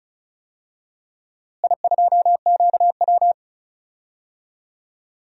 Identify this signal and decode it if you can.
Morse code, heard as a keyed tone, about 36 wpm, 700 Hz. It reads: I2QW
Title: DXCC-100 Call Signs - 4 Characters - Repeated 2x - 35wpm